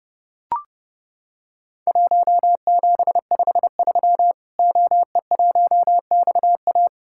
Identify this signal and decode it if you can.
Morse code, heard as a keyed tone, about 30 wpm, 700 Hz. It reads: 1753 OE1XA
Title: Worked All Europe DX Contest - QTC Exchange - 30wpm